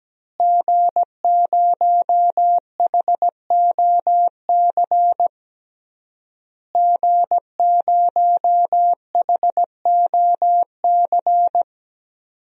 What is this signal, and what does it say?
Morse code, heard as a keyed tone, about 17 wpm, 700 Hz. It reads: G0HOC G0HOC
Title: DXCC-100 Call Signs - 5 Characters - Repeated 3x - 17wpm